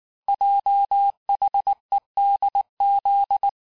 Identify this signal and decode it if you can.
Morse code, heard as a keyed tone, about 19 wpm, 800 Hz. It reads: JHEDZ
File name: cours-du-mercredi / audios / cours-869-texte-CHIFFRE-vitesse-18-MPM